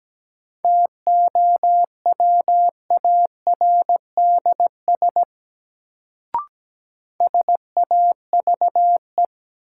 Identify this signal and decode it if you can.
Morse code, heard as a keyed tone, about 17 wpm, 700 Hz. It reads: TOWARDS SAVE
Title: Sentences from Top 1100 Words - New Word Drill Alt - 17wpm